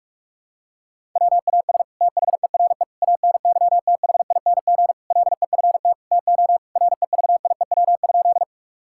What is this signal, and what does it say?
Morse code, heard as a keyed tone, about 46 wpm, 700 Hz. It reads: WAS THERE ANYTHING LEFT TO REVIEW?
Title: Sentences from Top 1000 Words - Review Part 6 - 45wpm